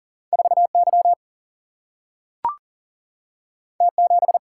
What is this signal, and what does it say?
Morse code, heard as a keyed tone, about 40 wpm, 700 Hz. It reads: 4 Y T 7
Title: Single-Letter Single-Number 40wpm